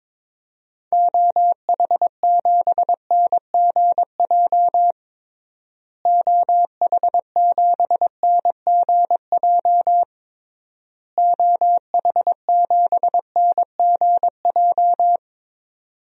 Morse code, heard as a keyed tone, about 22 wpm, 700 Hz. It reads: OH7NGJ OH7NGJ OH7NGJ
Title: DXCC-100 Call Signs - 6 Characters - Repeated 3x - 22wpm